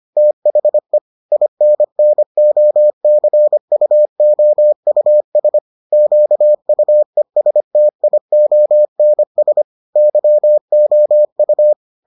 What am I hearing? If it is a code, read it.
Morse code, heard as a keyed tone, about 25 wpm, 600 Hz. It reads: THE INNOCUOUS QUESTIONS YOU